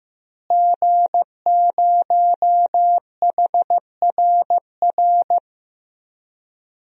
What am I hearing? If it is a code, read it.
Morse code, heard as a keyed tone, about 15 wpm, 700 Hz. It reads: G0HRR
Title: DXCC-100 Call Signs - 5 Characters - Repeated 3x - 15wpm